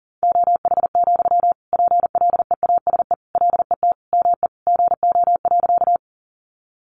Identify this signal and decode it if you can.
Morse code, heard as a keyed tone, about 40 wpm, 700 Hz. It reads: OH, PLEASE LET ME GO.
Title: Sentences from Top 1500 Words 40wpm